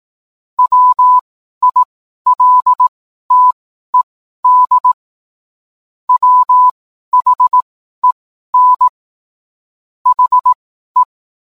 Morse code, heard as a keyed tone, about 18 wpm, spaced as for 12 wpm, 1000 Hz. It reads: WILTED WHEN HE